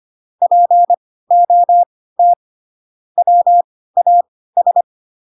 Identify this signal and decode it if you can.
Morse code, heard as a keyed tone, about 25 wpm, spaced as for 16 wpm, 700 Hz. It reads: POT WAS
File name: Pinocchio / Pinocchio ch5 10 25 ik1hge